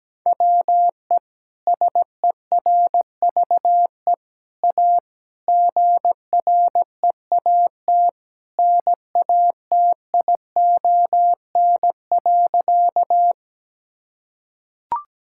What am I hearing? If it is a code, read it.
Morse code, heard as a keyed tone, about 17 wpm, 700 Hz. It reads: WE SERVE A GREAT NATION.
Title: Sentences from Top 1000 Words - Review Part 6 - 17wpm